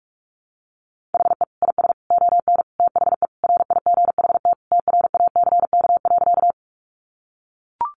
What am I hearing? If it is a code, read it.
Morse code, heard as a keyed tone, about 44 wpm, 700 Hz. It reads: HE IS ON THE RIGHT TRACK.
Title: Sentences from Top 1100 Words 45wpm